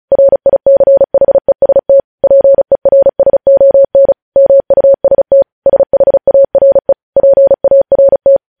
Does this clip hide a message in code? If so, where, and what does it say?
Yes: Morse code from 0.1 to 8.4 s, heard as a keyed tone, about 36 wpm, 550 Hz. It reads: RICHEST PERSON MUST SHARE PART